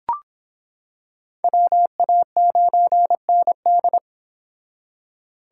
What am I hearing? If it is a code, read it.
Morse code, heard as a keyed tone, about 25 wpm, 700 Hz. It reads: WA9ND